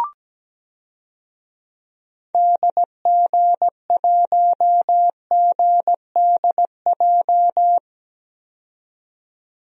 Morse code, heard as a keyed tone, about 17 wpm, 700 Hz. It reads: DG1GDJ
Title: DXCC-100 Call Signs - 6 Characters - 17wpm